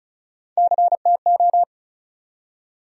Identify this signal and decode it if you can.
Morse code, heard as a keyed tone, about 34 wpm, 700 Hz. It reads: CTO
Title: USA Call Sign Postfixes - 3 Letters 35wpm